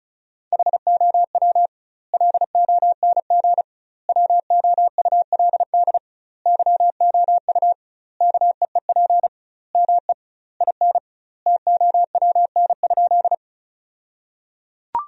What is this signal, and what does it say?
Morse code, heard as a keyed tone, about 36 wpm, 700 Hz. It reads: HOW LONG WOULD YOU KEEP ME IN TOWN?
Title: 2x Word-Spacing - Sentences from Top 500 Words - 35wpm